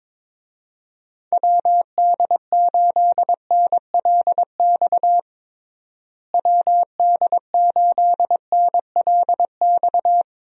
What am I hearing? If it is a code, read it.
Morse code, heard as a keyed tone, about 21 wpm, 700 Hz. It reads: WD8NLX WD8NLX
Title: USA Call Signs - 6 Characters - Repeated 2x - 22wpm